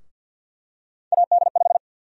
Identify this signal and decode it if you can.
Morse code, heard as a keyed tone, about 53 wpm, 700 Hz. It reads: AD5
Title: USA Call Sign Prefixes - Repeated 2x - 50wpm